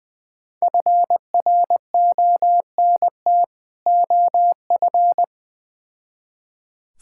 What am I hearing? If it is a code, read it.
Morse code, heard as a keyed tone, about 21 wpm, 700 Hz. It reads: FRONT OF
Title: Build A Phrase - 3-Word Phrases - 20wpm